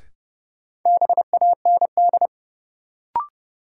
Morse code, heard as a keyed tone, about 30 wpm, 700 Hz. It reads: BAND
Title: Sentences from Top 1600 Words - New Word Drill - 30wpm